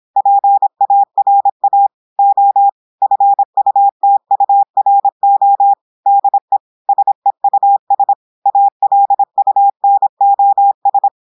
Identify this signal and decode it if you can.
Morse code, heard as a keyed tone, about 26 wpm, 800 Hz. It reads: PARA O FUTURO DE SEUS ALUNOS
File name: audios / portugais / portugais texte en clair 1 vitesse 26 mots minutes